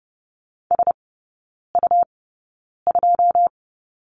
Morse code, heard as a keyed tone, about 30 wpm, spaced as for 9 wpm, 700 Hz. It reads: SU2
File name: Lesson 17e - Letter u - Three Characters 30wpm